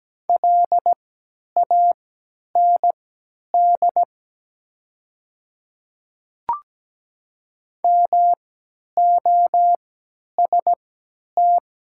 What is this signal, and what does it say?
Morse code, heard as a keyed tone, about 17 wpm, spaced as for 10 wpm, 700 Hz. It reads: LAND MOST
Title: 3x Character Spacing - Top 200 Words - 17wpm